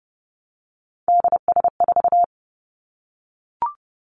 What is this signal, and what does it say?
Morse code, heard as a keyed tone, about 30 wpm, 700 Hz. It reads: DS4
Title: Three Challenging Letter-Number Combinations 30wpm